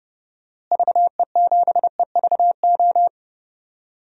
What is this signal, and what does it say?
Morse code, heard as a keyed tone, about 30 wpm, 700 Hz. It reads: VE7EVO
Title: DXCC-100 Call Signs - 6 Characters - Repeated 2x - 30wpm